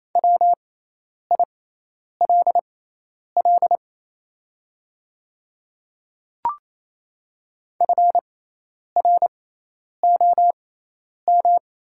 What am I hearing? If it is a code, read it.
Morse code, heard as a keyed tone, about 28 wpm, spaced as for 10 wpm, 700 Hz. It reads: WILL FROM
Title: 6x Character Spacing - Top 100 Words - 28wpm